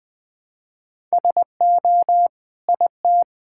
Morse code, heard as a keyed tone, about 20 wpm, 700 Hz. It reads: SO IT
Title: Sentences with 9-Word Phrases - Spoken Beginning - 20wpm